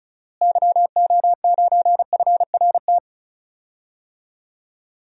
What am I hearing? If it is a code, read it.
Morse code, heard as a keyed tone, about 36 wpm, 700 Hz. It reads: YO9FRT
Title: DXCC-100 Call Signs - 6 Characters - Repeated 3x - 35wpm